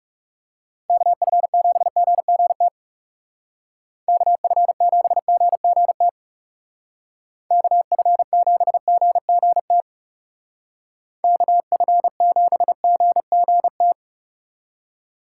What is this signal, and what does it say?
Morse code, heard as a keyed tone, about 37 wpm, 700 Hz. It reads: KF7GGT KF7GGT KF7GGT KF7GGT
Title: Speed Racing - USA 2x3 Call Signs 30wpm